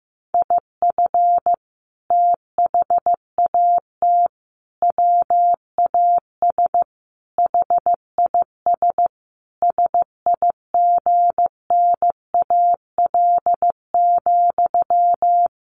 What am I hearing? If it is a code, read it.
Morse code, heard as a keyed tone, about 15 wpm, 700 Hz. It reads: IF THAT WAS HIS SIGNAL,
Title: Sentences from Top 1400 Words 15wpm